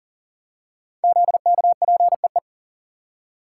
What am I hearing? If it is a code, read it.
Morse code, heard as a keyed tone, about 40 wpm, 700 Hz. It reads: ZKPEE
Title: Code Groups - 5 Letters - 40wpm